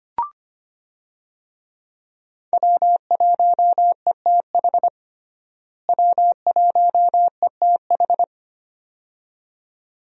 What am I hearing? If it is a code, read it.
Morse code, heard as a keyed tone, about 25 wpm, 700 Hz. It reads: W1ETH W1ETH
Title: USA Call Signs - 5 Characters - Repeated 2x - 25wpm